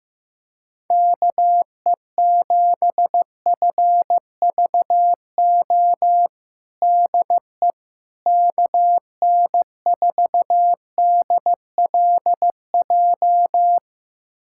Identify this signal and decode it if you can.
Morse code, heard as a keyed tone, about 15 wpm, 700 Hz. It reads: KE7FVO DE KN4DLJ
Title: USA Callsign DE Callsign - 6 Characters - 15wpm